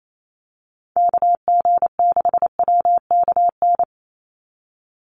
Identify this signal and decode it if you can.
Morse code, heard as a keyed tone, about 28 wpm, 700 Hz. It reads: KG6WKN
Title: USA Call Signs 28wpm